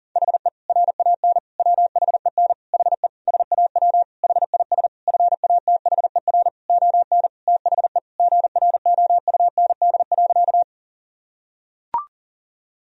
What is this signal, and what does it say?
Morse code, heard as a keyed tone, about 40 wpm, 700 Hz. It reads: HE RAN WHEN HE SAW HIS FATHER ON THE GROUND.
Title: Sentences from Top 1000 Words - Review Part 3 - 40wpm